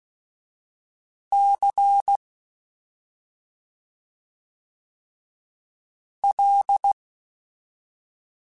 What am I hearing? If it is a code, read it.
Morse code, heard as a keyed tone, about 16 wpm, 800 Hz. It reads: C L
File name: Special INGs-mp3s / Special INGs-mp3s / Slow ing @10